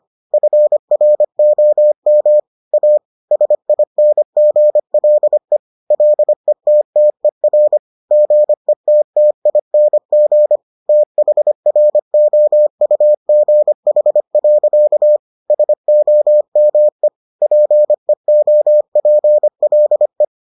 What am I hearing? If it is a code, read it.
Morse code, heard as a keyed tone, about 25 wpm, 600 Hz. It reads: FROM A SINGLE LETTER GETTING THROUGH. SOME PEOPLE